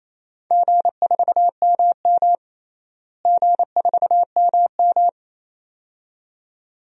Morse code, heard as a keyed tone, about 28 wpm, 700 Hz. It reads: G4MM G4MM
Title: DXCC-100 Call Signs - 4 Characters - Repeated 2x - 28wpm